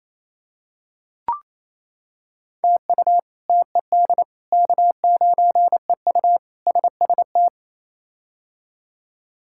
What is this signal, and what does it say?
Morse code, heard as a keyed tone, about 28 wpm, 700 Hz. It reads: TU TED K9EU SST
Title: K1USN SST QSO Practice - 28wpm